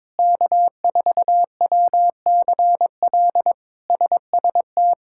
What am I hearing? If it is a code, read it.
Morse code, heard as a keyed tone, about 22 wpm, 700 Hz. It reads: K4WCL SST
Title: K1USN SST QSO Practice - 22wpm